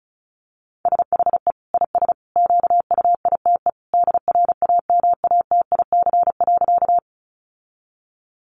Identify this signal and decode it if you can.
Morse code, heard as a keyed tone, about 34 wpm, 700 Hz. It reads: SHE IS QUITE DRAMATIC.
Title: Sentences from Top 1300 Words 35wpm